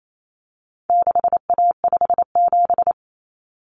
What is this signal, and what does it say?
Morse code, heard as a keyed tone, about 28 wpm, 700 Hz. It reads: 6A57
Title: Common Cut Numbers - 4-Digit 28wpm